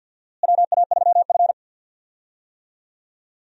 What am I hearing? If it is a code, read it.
Morse code, heard as a keyed tone, about 52 wpm, 700 Hz. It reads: WA3F
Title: USA Call Signs - 4 Characters - Repeated 3x - 50wpm